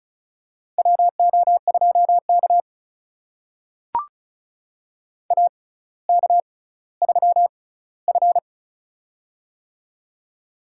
Morse code, heard as a keyed tone, about 36 wpm, spaced as for 18 wpm, 700 Hz. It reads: WO2K AK3F
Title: USA Call Signs - 4 Characters - 6x Character Spacing - 35wpm